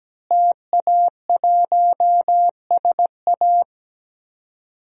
Morse code, heard as a keyed tone, about 17 wpm, 700 Hz. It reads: TA1SA
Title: DXCC-100 Call Signs - 5 Characters - Repeated 2x - 17wpm